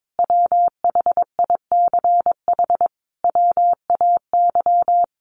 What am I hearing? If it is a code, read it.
Morse code, heard as a keyed tone, about 22 wpm, 700 Hz. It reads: WHICH WAY
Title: Sentences from Top 1300 Words 22wpm